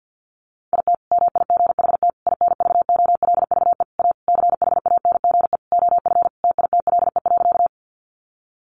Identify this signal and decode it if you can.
Morse code, heard as a keyed tone, about 49 wpm, 700 Hz. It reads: IT MIGHT INVOLVE A CHANGE OF TITLE.
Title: Sentences from Top 1200 Words 50wpm